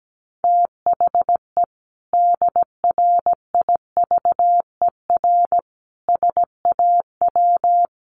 Morse code, heard as a keyed tone, about 17 wpm, 700 Hz. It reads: THE DRIVER SAW
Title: Sentences from Top 1000 Words 17wpm